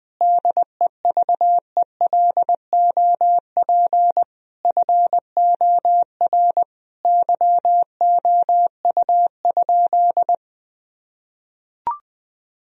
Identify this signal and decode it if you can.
Morse code, heard as a keyed tone, about 20 wpm, 700 Hz. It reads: DEVELOP FOR YOU?
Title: Sentences from Top 1000 Words - Review Part 2 - 20wpm